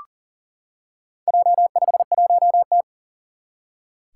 Morse code, heard as a keyed tone, about 40 wpm, 700 Hz. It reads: J51T
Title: Four Challenging Letter-Number Combinations 40wpm